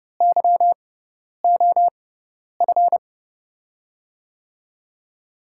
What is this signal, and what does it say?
Morse code, heard as a keyed tone, about 31 wpm, spaced as for 10 wpm, 700 Hz. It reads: YOF
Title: USA Call Signs - 6x Character Spacing - 30wpm